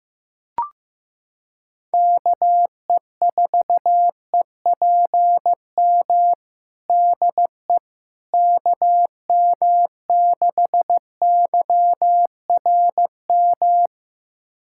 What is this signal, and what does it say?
Morse code, heard as a keyed tone, about 15 wpm, 700 Hz. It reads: KE4EPM DE KM6YRM
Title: USA Callsign DE Callsign - 6 Characters - 15wpm